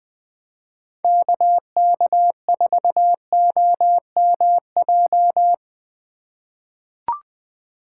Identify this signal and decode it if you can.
Morse code, heard as a keyed tone, about 20 wpm, 700 Hz. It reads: KK4OMJ
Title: 2023 POTA Hunter Call Signs - 20wpm